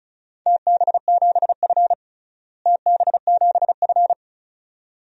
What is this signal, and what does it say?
Morse code, heard as a keyed tone, about 34 wpm, 700 Hz. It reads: TB7F TB7F
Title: DXCC-100 Call Signs - 4 Characters - Repeated 2x - 35wpm